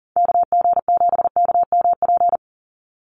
Morse code, heard as a keyed tone, about 40 wpm, 700 Hz. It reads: KG7KMP
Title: USA Call Signs 40wpm